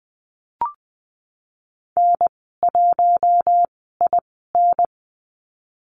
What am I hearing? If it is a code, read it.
Morse code, heard as a keyed tone, about 20 wpm, spaced as for 14 wpm, 700 Hz. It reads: N1IN
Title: 069 Lesson 12g - Letter d - Call Signs 20wpm